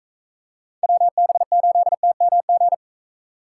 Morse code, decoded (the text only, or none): WB8TMG